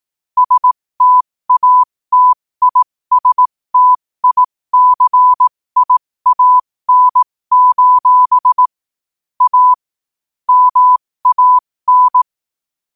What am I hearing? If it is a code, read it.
Morse code, heard as a keyed tone, about 18 wpm, spaced as for 15 wpm, 1000 Hz. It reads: STATISTICIAN: A MAN
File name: qotdfiles / 15wpm